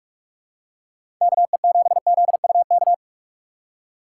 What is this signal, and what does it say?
Morse code, heard as a keyed tone, about 46 wpm, 700 Hz. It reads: KE7ZUK
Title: USA Call Signs - 6 Characters - 4x Character Spacing - 45wpm